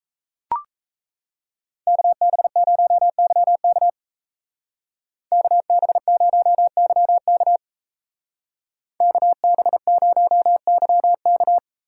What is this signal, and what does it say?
Morse code, heard as a keyed tone, about 34 wpm, 700 Hz. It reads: KB0YK KB0YK KB0YK
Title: Speed Racing - USA 2x2 Call Signs 28wpm